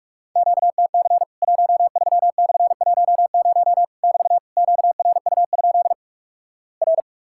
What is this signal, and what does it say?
Morse code, heard as a keyed tone, about 46 wpm, 700 Hz. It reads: QTC 13/10 <BT> QRU? R
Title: Worked All Europe DX Contest - QTC Exchange - 45wpm